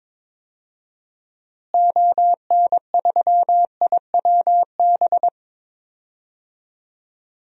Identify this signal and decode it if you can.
Morse code, heard as a keyed tone, about 22 wpm, 700 Hz. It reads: ON3IWB
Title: DXCC-100 Call Signs - 6 Characters - Repeated 3x - 22wpm